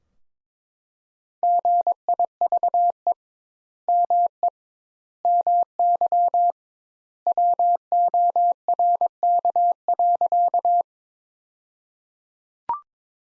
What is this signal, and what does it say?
Morse code, heard as a keyed tone, about 21 wpm, 700 Hz. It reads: GIVE ME MY WORK.
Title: Sentences Top 200 Words - 22wpm